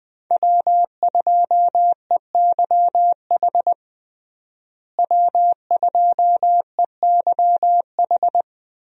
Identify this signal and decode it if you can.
Morse code, heard as a keyed tone, about 20 wpm, 700 Hz. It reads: W2EYH W2EYH
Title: USA Call Signs - 5 Characters - Repeated 2x - 20wpm